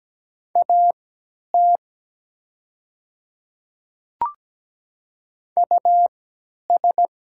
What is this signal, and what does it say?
Morse code, heard as a keyed tone, about 17 wpm, spaced as for 10 wpm, 690 Hz. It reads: AT US
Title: 3x Character Spacing - Top 300 Words - 17wpm